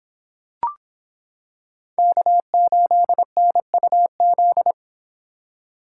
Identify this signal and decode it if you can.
Morse code, heard as a keyed tone, about 25 wpm, 700 Hz. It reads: K8NUZ